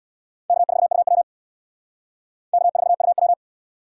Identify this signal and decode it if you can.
Morse code, heard as a keyed tone, about 64 wpm, 700 Hz. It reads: K4UW K4UW